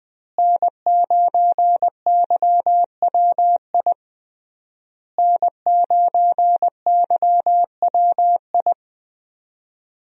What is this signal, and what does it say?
Morse code, heard as a keyed tone, about 20 wpm, 700 Hz. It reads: N9YWI N9YWI
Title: USA Call Signs - 5 Characters - Repeated 3x - 20wpm